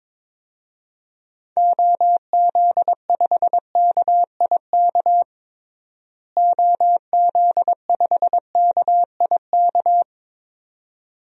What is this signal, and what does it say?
Morse code, heard as a keyed tone, about 22 wpm, 700 Hz. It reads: OZ5KIK OZ5KIK
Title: DXCC-100 Call Signs - 6 Characters - Repeated 2x - 22wpm